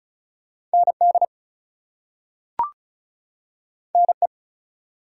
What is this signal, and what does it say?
Morse code, heard as a keyed tone, about 36 wpm, 700 Hz. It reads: ND NE
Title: US States and Canadian Provinces 35wpm